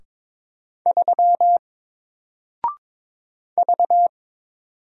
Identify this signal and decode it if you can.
Morse code, heard as a keyed tone, about 22 wpm, 700 Hz. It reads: 3 V
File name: Single-Letter-Number-22wpm